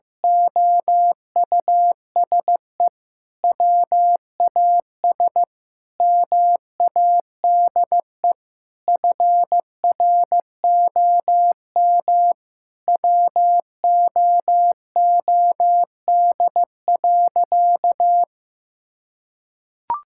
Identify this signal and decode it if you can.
Morse code, heard as a keyed tone, about 15 wpm, 700 Hz. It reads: OUSE WAS MADE FROM WOOD.
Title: Sentences from Top 1000 Words - Review Part 6 - 15wpm